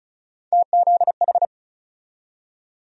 Morse code, heard as a keyed tone, about 36 wpm, 700 Hz. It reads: TZH
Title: USA Call Sign Postfixes - 3 Letters 35wpm